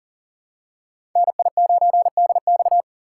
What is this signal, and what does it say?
Morse code, heard as a keyed tone, about 43 wpm, 700 Hz. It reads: NI9DX
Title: USA Call Signs - 8x Character Spacing - 40wpm